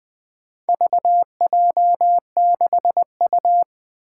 Morse code, heard as a keyed tone, about 20 wpm, 700 Hz. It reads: VJ6U
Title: Four Challenging Letter-Number Combinations 20wpm